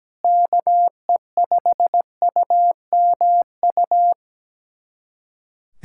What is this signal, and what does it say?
Morse code, heard as a keyed tone, about 17 wpm, 700 Hz. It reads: KE5UMU
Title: USA Callsign DE Callsign 17wpm